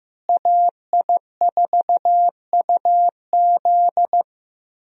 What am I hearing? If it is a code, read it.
Morse code, heard as a keyed tone, about 16 wpm, 700 Hz. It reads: AI4UZ
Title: USA Call Signs - 5 Characters - Repeated 3x - 15wpm